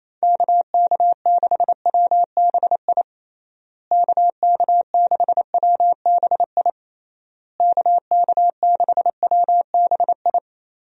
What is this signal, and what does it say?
Morse code, heard as a keyed tone, about 28 wpm, 700 Hz. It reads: KK6WBI KK6WBI KK6WBI
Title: USA Call Signs - 6 Characters - Repeated 3x - 28wpm